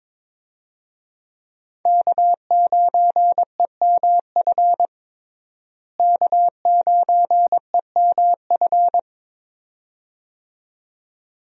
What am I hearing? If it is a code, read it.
Morse code, heard as a keyed tone, about 22 wpm, 700 Hz. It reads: K9EMF K9EMF
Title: USA Call Signs - 5 Characters - Repeated 2x - 22wpm